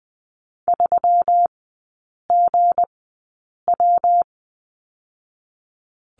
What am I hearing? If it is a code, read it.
Morse code, heard as a keyed tone, about 20 wpm, spaced as for 8 wpm, 700 Hz. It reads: 3GW
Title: 218 Lesson 35e - Number 0 - Three Characters 20wpm